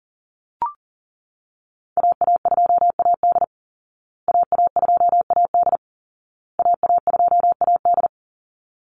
Morse code, heard as a keyed tone, about 40 wpm, 700 Hz. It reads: AA2AD AA2AD AA2AD
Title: Call Signs Repeated 2x - 40wpm